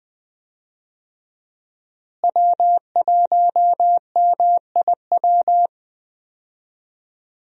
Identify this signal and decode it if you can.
Morse code, heard as a keyed tone, about 20 wpm, 700 Hz. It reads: W1MIW
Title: USA Call Signs - 5 Characters - Repeated 3x - 20wpm